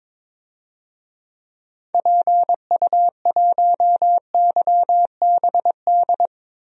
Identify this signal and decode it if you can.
Morse code, heard as a keyed tone, about 22 wpm, 700 Hz. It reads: PU1YBD
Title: DXCC-100 Call Signs - 6 Characters - 22wpm